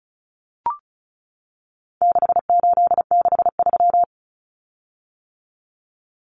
Morse code, heard as a keyed tone, about 34 wpm, 700 Hz. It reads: 6863
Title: Common Cut Numbers - 4-Digit 35wpm